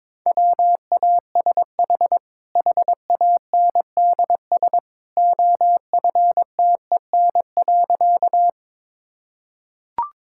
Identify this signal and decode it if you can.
Morse code, heard as a keyed tone, about 21 wpm, 700 Hz. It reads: WASH HANDS OFTEN.
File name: practice / 22wpm / Easy Sentences with Hints - 22wpm